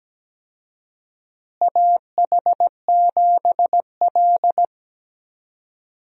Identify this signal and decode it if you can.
Morse code, heard as a keyed tone, about 17 wpm, 700 Hz. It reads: AH7L
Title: USA Call Signs - 4 Characters - Repeated 3x - 17wpm